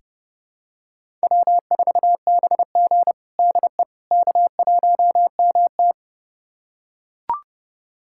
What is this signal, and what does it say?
Morse code, heard as a keyed tone, about 30 wpm, 700 Hz. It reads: W4BG DE K1MT
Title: USA Callsign DE Callsign - 4 Characters - 30wpm